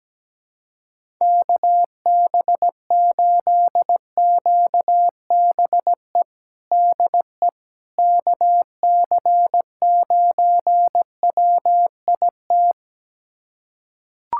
Morse code, heard as a keyed tone, about 17 wpm, 700 Hz. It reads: KB8QBE DE KC9WIT
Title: USA Callsign DE Callsign 17wpm